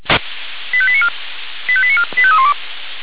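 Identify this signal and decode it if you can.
Five-tone selective call (ZVEI), heard as quick runs of five tones, tones of about 70 ms. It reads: 85894 85894 85312
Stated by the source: ZVEI five-tone call 85894, received on SDR.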